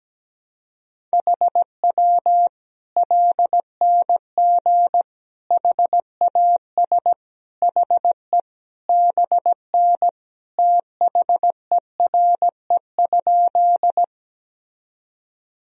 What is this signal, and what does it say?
Morse code, heard as a keyed tone, about 17 wpm, 700 Hz. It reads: HW LNG HAS HE BN THERE?
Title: Sentences from Top 100 Words and Abbreviations 17wpm